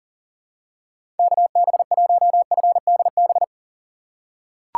Morse code, heard as a keyed tone, about 40 wpm, 700 Hz. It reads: KB1FDB
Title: USA Call Signs - 6 Characters - 4x Character Spacing - 40wpm